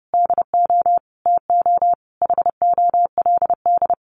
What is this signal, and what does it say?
Morse code, heard as a keyed tone, about 30 wpm, 700 Hz. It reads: DO TO HOLD
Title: Sentences from Top 400 Words 30wpm